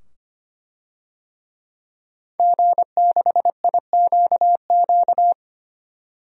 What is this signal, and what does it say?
Morse code, heard as a keyed tone, about 25 wpm, 700 Hz. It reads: G6IQQ
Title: DXCC-100 Call Signs - 5 Characters - 25wpm